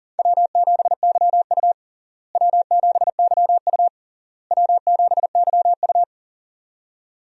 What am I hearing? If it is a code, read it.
Morse code, heard as a keyed tone, about 40 wpm, 700 Hz. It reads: W7YU W7YU W7YU
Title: USA Call Signs - 4 Characters - Repeated 3x - 40wpm